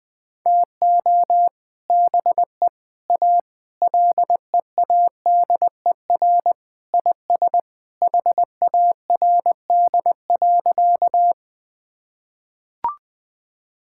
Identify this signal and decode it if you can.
Morse code, heard as a keyed tone, about 20 wpm, 700 Hz. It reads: TO BE A LEADER IS HARD.
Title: Sentences with 3-Word Phrases - 20wpm